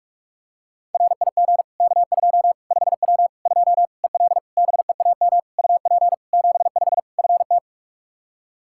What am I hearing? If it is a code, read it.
Morse code, heard as a keyed tone, about 41 wpm, 700 Hz. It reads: RIG K2 5W 2 EL BEAM UP 75 FT
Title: Practice QSO - Exchange 2 - Info - 45wpm